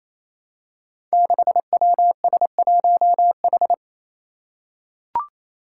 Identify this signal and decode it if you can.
Morse code, heard as a keyed tone, about 27 wpm, 700 Hz. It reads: 6WS1H
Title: Five Challenging Letter-Number Combinations 28wpm